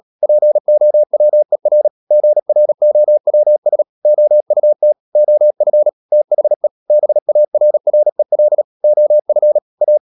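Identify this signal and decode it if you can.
Morse code, heard as a keyed tone, about 37 wpm, 600 Hz. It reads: POWER GROWS OUT OF THE BARREL OF A
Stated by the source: Morse code practice at 35–39 wpm sending English text of Ken Follett's novel Never, from CW med Gnister.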